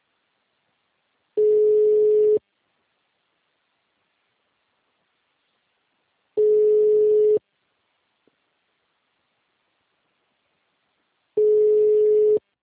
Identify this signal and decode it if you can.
Ringback tone (Europe, 425 Hz).